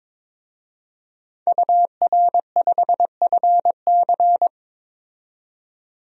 Morse code, heard as a keyed tone, about 22 wpm, 700 Hz. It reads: UR5FC
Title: DXCC-100 Call Signs - 5 Characters - 22wpm